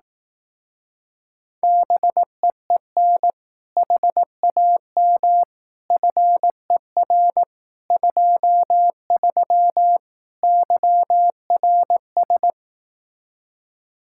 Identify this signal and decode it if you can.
Morse code, heard as a keyed tone, about 18 wpm, 700 Hz. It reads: BEEN HAM FER 23 YRS
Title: Speed-Racing Common QSO Phrases Encore - 15wpm